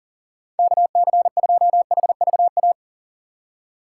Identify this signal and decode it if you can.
Morse code, heard as a keyed tone, about 40 wpm, 700 Hz. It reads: KC2HVA
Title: USA Call Signs - 6 Characters - 4x Character Spacing - 40wpm